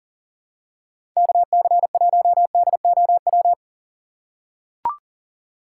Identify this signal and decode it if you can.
Morse code, heard as a keyed tone, about 39 wpm, 700 Hz. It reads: KC1DOW
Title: USA Call Signs - 6 Characters - 3x Character Spacing - 40wpm